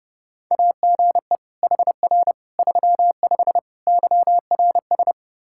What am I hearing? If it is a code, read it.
Morse code, heard as a keyed tone, about 30 wpm, 700 Hz. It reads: AGE HR 35 YRS
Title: Practice QSO - Exchange 3 - More Info - 30wpm